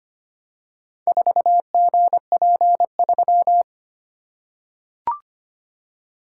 Morse code, heard as a keyed tone, about 25 wpm, 700 Hz. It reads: 4GP3
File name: practice / 25wpm / Four Letter-Number Combinations - 25wpm